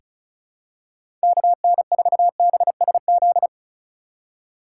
Morse code, heard as a keyed tone, about 36 wpm, 700 Hz. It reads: KN4BSZ
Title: USA Call Signs - 6 Characters - 3x Character Spacing - 35wpm